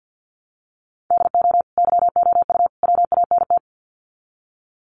Morse code, heard as a keyed tone, about 50 wpm, 700 Hz. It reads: DO YOU WANT
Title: Sets of 3 Words 50wpm